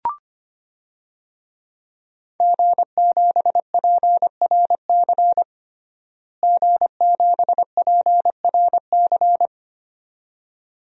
Morse code, heard as a keyed tone, about 25 wpm, 700 Hz. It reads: G7PRC G7PRC
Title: DXCC-100 Call Signs - 5 Characters - Repeated 2x - 25wpm